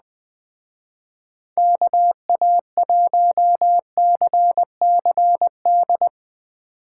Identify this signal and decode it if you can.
Morse code, heard as a keyed tone, about 20 wpm, 700 Hz. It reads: KA1CCD